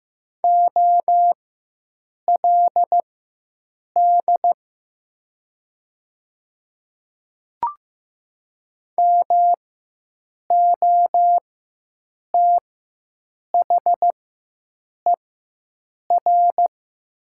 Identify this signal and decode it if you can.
Morse code, heard as a keyed tone, about 15 wpm, spaced as for 7 wpm, 700 Hz. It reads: OLD MOTHER
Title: 4x Character Spacing - Top 200 Words - 15wpm